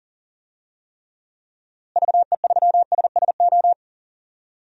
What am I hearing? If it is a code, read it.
Morse code, heard as a keyed tone, about 40 wpm, 700 Hz. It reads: VE3SSO
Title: DXCC-100 Call Signs - 6 Characters - Repeated 3x - 40wpm